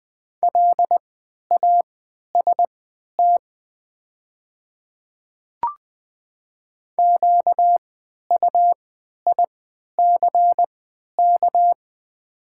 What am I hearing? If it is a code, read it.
Morse code, heard as a keyed tone, about 20 wpm, spaced as for 11 wpm, 700 Hz. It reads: LAST QUICK
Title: 3x Character Spacing - Top 500 Words - 20wpm